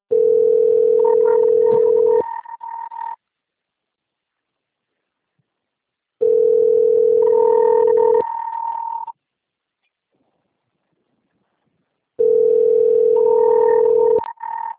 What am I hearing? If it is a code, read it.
Ringback tone (North America), 440 + 480 Hz.
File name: out- - - - - .88581